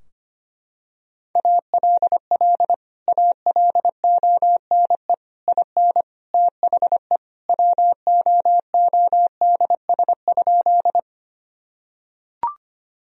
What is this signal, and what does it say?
Morse code, heard as a keyed tone, about 25 wpm, 700 Hz. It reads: ALL ALONE IN THE WOODS?
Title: Sentences from Top 1000 Words - Review Part 2 - 25wpm